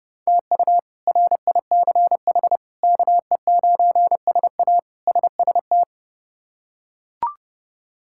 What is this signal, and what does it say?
Morse code, heard as a keyed tone, about 30 wpm, 700 Hz. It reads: TU RICH KE9SA SST
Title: K1USN SST QSO Practice - 30wpm